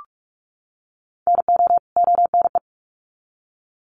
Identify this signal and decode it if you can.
Morse code, heard as a keyed tone, about 44 wpm, 700 Hz. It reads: NO ONE
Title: Sets of 2 Words 45wpm